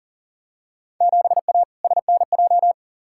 Morse code, heard as a keyed tone, about 40 wpm, 700 Hz. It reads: 7A SNJ
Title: Field Day Exchanges Encore - 40wpm